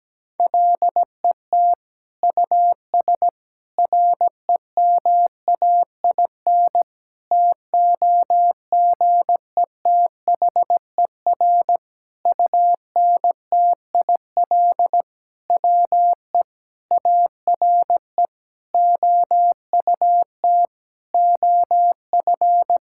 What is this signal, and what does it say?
Morse code, heard as a keyed tone, about 17 wpm, 700 Hz. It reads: LET US REMAIN TOGETHER UNTIL WE ARE OUT OF
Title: Sentences from Top 1600 Words 17wpm